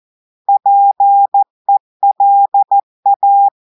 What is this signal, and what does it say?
Morse code, heard as a keyed tone, about 14 wpm, 800 Hz. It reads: PELA
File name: audios / portugais / portugais texte en clair 1 vitesse 14 mots minutes